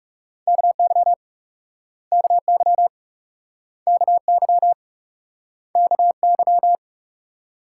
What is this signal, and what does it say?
Morse code, heard as a keyed tone, about 42 wpm, 700 Hz. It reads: KY KY KY KY